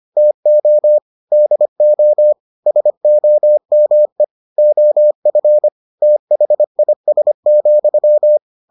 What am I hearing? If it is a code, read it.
Morse code, heard as a keyed tone, about 25 wpm, 600 Hz. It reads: TO DO SOME OF THIS,